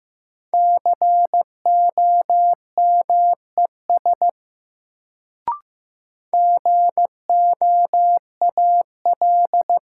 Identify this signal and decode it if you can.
Morse code, heard as a keyed tone, about 15 wpm, 700 Hz. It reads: COMES GOAL
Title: Sentences from Top 1000 Words - Review Part 5 - New Word Drill - 15wpm